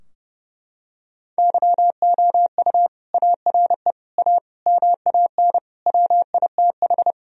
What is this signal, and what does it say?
Morse code, heard as a keyed tone, about 30 wpm, 700 Hz. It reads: YOU ARE A MAN WITH
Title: Sentences from Top 1000 Words - Review Part 3 - 30wpm